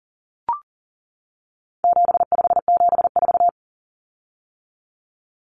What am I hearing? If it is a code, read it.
Morse code, heard as a keyed tone, about 40 wpm, 700 Hz. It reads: 7574
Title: Common Cut Numbers - 4-Digit 40wpm